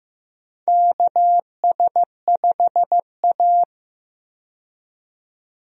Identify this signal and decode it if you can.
Morse code, heard as a keyed tone, about 15 wpm, 700 Hz. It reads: KS5A
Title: USA Call Signs - 4 Characters - 15wpm